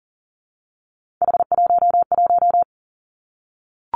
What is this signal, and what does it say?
Morse code, heard as a keyed tone, about 40 wpm, 700 Hz. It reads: H11
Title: Three Challenging Letter-Number Combinations 40wpm